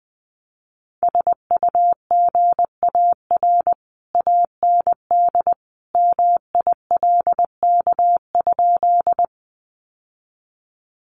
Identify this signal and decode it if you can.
Morse code, heard as a keyed tone, about 20 wpm, 700 Hz. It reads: SUGAR AND MILK?
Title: Sentences from Top 1500 Words 20wpm